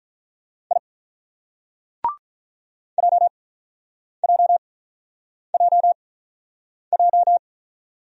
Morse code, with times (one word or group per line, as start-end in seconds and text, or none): 0.7-0.8 I
3.0-7.4 JJJJ